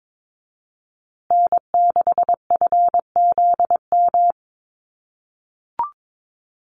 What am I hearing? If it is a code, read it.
Morse code, heard as a keyed tone, about 22 wpm, 700 Hz. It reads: N6FZM
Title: USA Call Signs Repeated 2x 22wpm